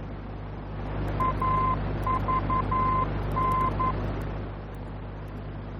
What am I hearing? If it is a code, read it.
Morse code, heard as a keyed tone, about 11 wpm, 1030 Hz, with noise. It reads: AVN
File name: france / mx / AVN-F-VOR